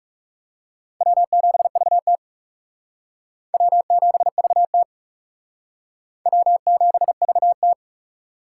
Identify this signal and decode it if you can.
Morse code, heard as a keyed tone, about 38 wpm, 700 Hz. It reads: W7VT W7VT W7VT
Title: Speed Racing - USA 1x2 Call Signs 30wpm